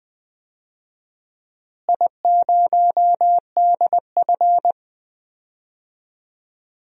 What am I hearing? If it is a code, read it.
Morse code, heard as a keyed tone, about 20 wpm, 700 Hz. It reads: I0DF
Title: DXCC-100 Call Signs - 4 Characters - Repeated 3x - 20wpm